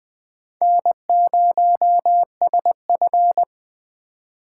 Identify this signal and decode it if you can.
Morse code, heard as a keyed tone, about 21 wpm, 700 Hz. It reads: N0SF